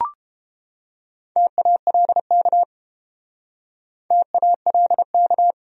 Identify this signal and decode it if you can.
Morse code, heard as a keyed tone, about 32 wpm, 700 Hz. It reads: TALK TALK